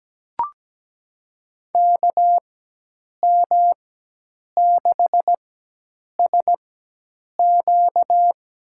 Morse code, heard as a keyed tone, about 17 wpm, spaced as for 8 wpm, 700 Hz. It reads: KM6SQ
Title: USA Call Signs - 4x Character Spacing - 17wpm